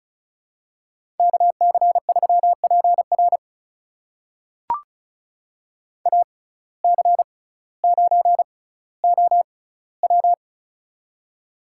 Morse code, heard as a keyed tone, about 35 wpm, spaced as for 18 wpm, 700 Hz. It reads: KC3PR AC9OW
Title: USA Call Signs - 5 Characters - 6x Character Spacing - 35wpm